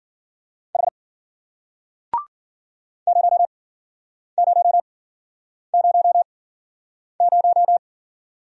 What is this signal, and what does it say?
Morse code, heard as a keyed tone, about 51 wpm, spaced as for 9 wpm, 700 Hz. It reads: H 0000